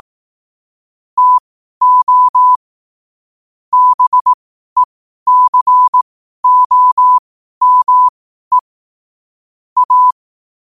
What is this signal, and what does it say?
Morse code, heard as a keyed tone, about 18 wpm, spaced as for 12 wpm, 1000 Hz. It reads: TO BECOME A
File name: qotdfiles / 12wpm